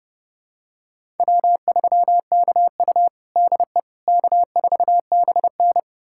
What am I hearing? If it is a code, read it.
Morse code, heard as a keyed tone, about 32 wpm, 700 Hz. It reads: W3KU DE K4BN
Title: USA Callsign DE Callsign - 4 Characters - 30wpm